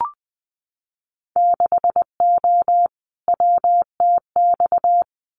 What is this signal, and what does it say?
Morse code, heard as a keyed tone, about 20 wpm, 700 Hz. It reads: 6O WTX
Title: Winter Field Day Exchanges Encore 20wpm